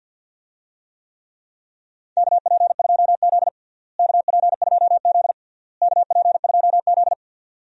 Morse code, heard as a keyed tone, about 51 wpm, 700 Hz. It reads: KP2Z KP2Z KP2Z
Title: DXCC-100 Call Signs - 4 Characters - Repeated 3x - 50wpm